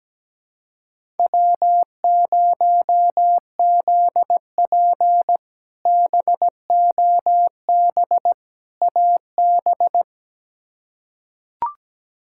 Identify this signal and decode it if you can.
Morse code, heard as a keyed tone, about 17 wpm, 700 Hz. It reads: W0ZP BOB AB
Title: K1USN SST QSO Practice - 17wpm